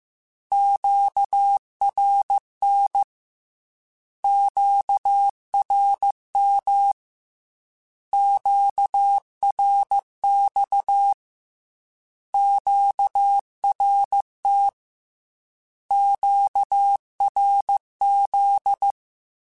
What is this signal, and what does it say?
Morse code, heard as a keyed tone, about 15 wpm, 800 Hz. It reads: QRN QRM QRX QRT QRZ
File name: Q Calls / Q Calls-mp3s / Q Calls Once @13